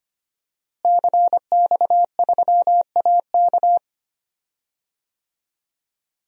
Morse code, heard as a keyed tone, about 25 wpm, 700 Hz. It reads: CX3AK
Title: DXCC-100 Call Signs - 5 Characters - Repeated 2x - 25wpm